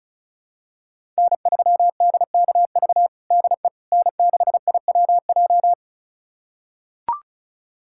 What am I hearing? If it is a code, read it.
Morse code, heard as a keyed tone, about 36 wpm, 700 Hz. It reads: N3DKV DE N6IWJ
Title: USA Callsign DE Callsign - 5 Characters - 35wpm